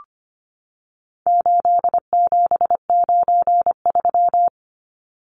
Morse code, heard as a keyed tone, about 25 wpm, 700 Hz. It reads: 8793
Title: Numbers - 4-Digit 25wpm